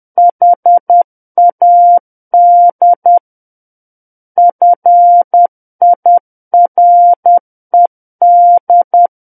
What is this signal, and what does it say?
Morse code, heard as a keyed tone, about 10 wpm, 700 Hz. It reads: HAD FIRED
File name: Book One / 01-Chapter 1 - 10 WPM